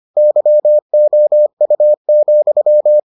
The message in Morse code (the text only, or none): YOU,